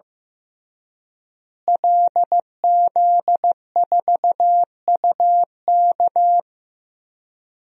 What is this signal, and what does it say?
Morse code, heard as a keyed tone, about 15 wpm, 700 Hz. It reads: LZ4UK
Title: DXCC-100 Call Signs - 5 Characters - Repeated 3x - 15wpm